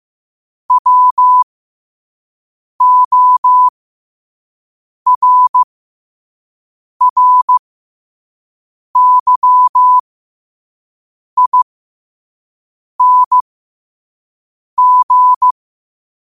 Morse code, heard as a keyed tone, about 15 wpm, spaced as for 5 wpm, 1000 Hz. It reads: WORRYING
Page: 5 WPM morse code quotes for Wed, 30 Jul 2025.